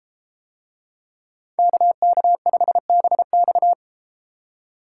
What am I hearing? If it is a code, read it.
Morse code, heard as a keyed tone, about 32 wpm, 700 Hz. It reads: KK5BX